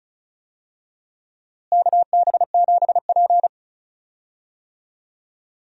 Morse code, heard as a keyed tone, about 34 wpm, 700 Hz. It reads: KB7P
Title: USA Call Signs - 4 Characters - Repeated 3x - 35wpm